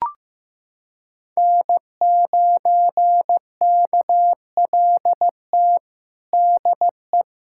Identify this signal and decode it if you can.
Morse code, heard as a keyed tone, about 15 wpm, 700 Hz. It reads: N9KLT DE
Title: USA Callsign DE Callsign - 5 Characters - 15wpm